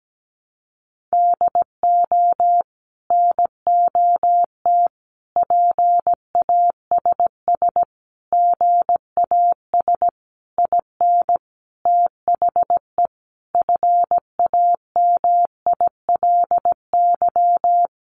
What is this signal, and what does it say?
Morse code, heard as a keyed tone, about 17 wpm, 700 Hz. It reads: DO NOT PASS GAS IN THE FAMILY
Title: Sentences from Top 1100 Words 17wpm